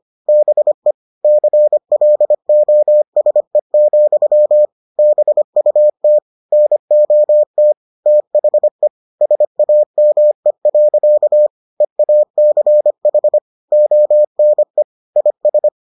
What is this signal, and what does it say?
Morse code, heard as a keyed tone, about 25 wpm, 600 Hz. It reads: BE CLOSE, BUT NOT THE SAME. EACH ONE IS